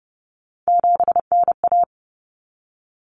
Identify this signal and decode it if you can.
Morse code, heard as a keyed tone, about 30 wpm, 700 Hz. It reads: 7NA
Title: Common Cut Numbers - 3-Digit 30wpm